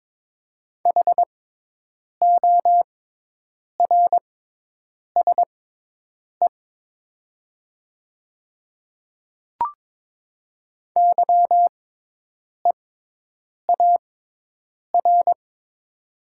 Morse code, heard as a keyed tone, about 22 wpm, spaced as for 8 wpm, 700 Hz. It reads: HORSE YEAR
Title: 6x Character Spacing - Top 300 Words - 22wpm